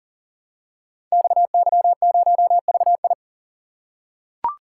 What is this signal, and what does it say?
Morse code, heard as a keyed tone, about 39 wpm, 700 Hz. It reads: XY0VI
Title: Code Groups - 5 Letter:Numbers - 40wpm